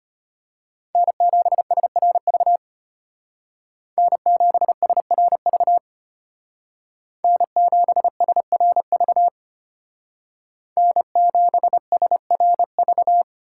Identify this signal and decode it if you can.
Morse code, heard as a keyed tone, about 32 wpm, 700 Hz. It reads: N7SRV N7SRV N7SRV N7SRV